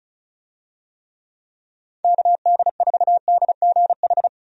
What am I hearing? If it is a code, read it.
Morse code, heard as a keyed tone, about 34 wpm, 700 Hz. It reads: KD4DGH
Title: USA Call Signs - 6 Characters - 35wpm